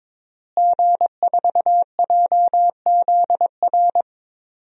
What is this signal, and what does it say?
Morse code, heard as a keyed tone, about 22 wpm, 700 Hz. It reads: G4JZR
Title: DXCC-100 Call Signs - 5 Characters - Repeated 3x - 22wpm